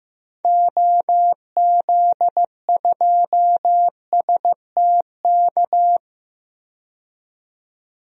Morse code, heard as a keyed tone, about 16 wpm, 700 Hz. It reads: OZ2STK
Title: DXCC-100 Call Signs - 6 Characters - 15wpm